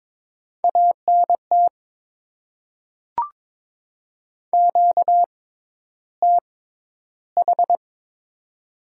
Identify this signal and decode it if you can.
Morse code, heard as a keyed tone, about 22 wpm, spaced as for 11 wpm, 700 Hz. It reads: ANT QTH
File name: Top 20 QSO Elements - 6x Character Spacing - 22wpm